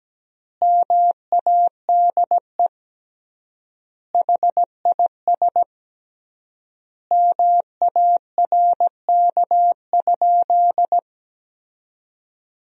Morse code, heard as a keyed tone, about 17 wpm, 700 Hz. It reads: MADE HIS MARK?
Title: 3x Word-Spacing - Sentences from Top 300 Words - 17wpm